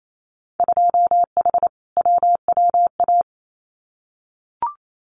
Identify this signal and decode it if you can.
Morse code, heard as a keyed tone, about 28 wpm, 700 Hz. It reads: 2H WWA
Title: Winter Field Day Exchanges 28wpm